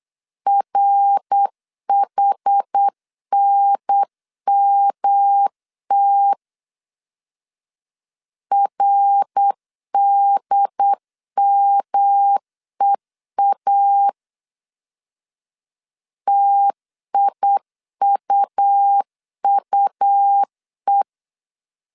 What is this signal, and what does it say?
Morse code, heard as a keyed tone, about 8 wpm, 800 Hz. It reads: RHNMT RDMEA TIUUE
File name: CW / DET / 1 3 1500C S B 8wpm 4141